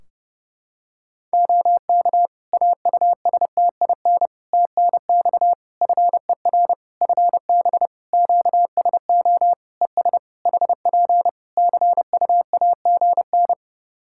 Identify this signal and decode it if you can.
Morse code, heard as a keyed tone, about 29 wpm, 700 Hz. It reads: OK AUSTIN TNX FER FB QSO ES HP CUAGN
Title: Practice QSO - Exchange 4 - Ending - 30wpm